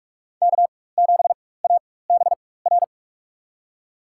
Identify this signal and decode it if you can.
Morse code, heard as a keyed tone, about 42 wpm, spaced as for 20 wpm, 700 Hz. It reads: K7ABR
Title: USA Call Signs - 5 Characters - 4x Character Spacing - 45wpm